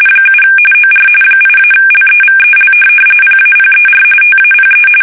RTTY, about 45.45 baud. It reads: 7 OR 8-LEVEL PAPER TAPE
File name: fsk850 / teletype-corp